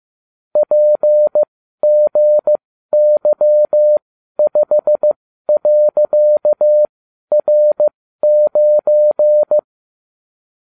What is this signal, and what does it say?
Morse code, heard as a keyed tone, about 15 wpm, spaced as for 12 wpm, 600 Hz. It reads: PGY5.R9